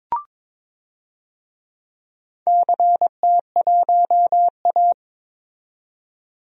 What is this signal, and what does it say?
Morse code, heard as a keyed tone, about 22 wpm, 700 Hz. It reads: CT1A
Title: DXCC-100 Call Signs - 4 Characters - 22wpm